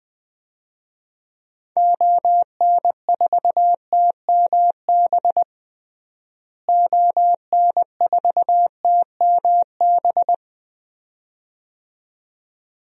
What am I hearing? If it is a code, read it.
Morse code, heard as a keyed tone, about 21 wpm, 700 Hz. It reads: ON4TMB ON4TMB
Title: DXCC-100 Call Signs - 6 Characters - Repeated 2x - 20wpm